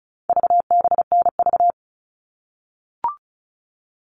Morse code, heard as a keyed tone, about 34 wpm, 700 Hz. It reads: VBNV
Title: All Cut Numbers - 4-Digit 35wpm